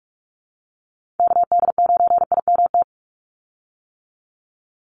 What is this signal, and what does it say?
Morse code, heard as a keyed tone, about 44 wpm, 700 Hz. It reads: KD9IMT
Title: USA Call Signs 45wpm